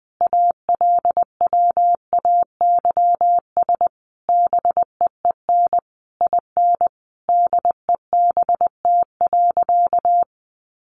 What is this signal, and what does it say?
Morse code, heard as a keyed tone, about 20 wpm, 700 Hz. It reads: ALWAYS BEEN IN DEBT.
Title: Sentences from Top 1300 Words 20wpm